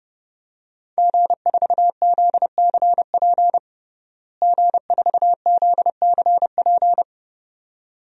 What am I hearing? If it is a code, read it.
Morse code, heard as a keyed tone, about 32 wpm, 700 Hz. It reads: G4ZCP G4ZCP
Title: DXCC-100 Call Signs - 5 Characters - Repeated 2x - 30wpm